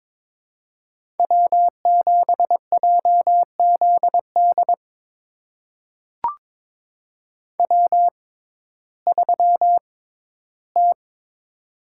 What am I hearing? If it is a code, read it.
Morse code, heard as a keyed tone, about 22 wpm, spaced as for 13 wpm, 700 Hz. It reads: W7JZD W3T
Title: USA Call Signs - 5 Characters - 6x Character Spacing - 22wpm